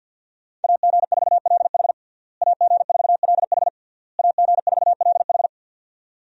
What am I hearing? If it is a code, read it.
Morse code, heard as a keyed tone, about 52 wpm, 700 Hz. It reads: AG4LH AG4LH AG4LH
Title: USA Call Signs - 5 Characters - Repeated 3x - 50wpm